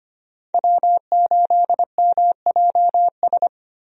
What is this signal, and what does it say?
Morse code, heard as a keyed tone, about 25 wpm, 700 Hz. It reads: W8MJS
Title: USA Call Signs - 5 Characters - Repeated 2x - 25wpm